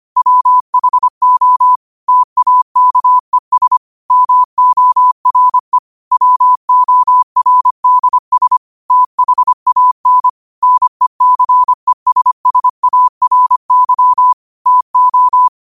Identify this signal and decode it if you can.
Morse code, heard as a keyed tone, about 25 wpm, 1000 Hz. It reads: WHO TAKES MORE WORDS THAN NECESSARY TO
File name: qotdfiles / 25wpm